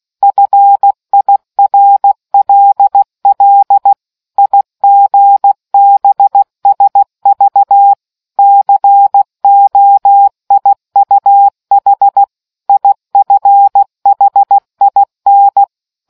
Morse code, heard as a keyed tone, about 16 wpm, 800 Hz. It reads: FIRLL IGBSV COIUH IFHIN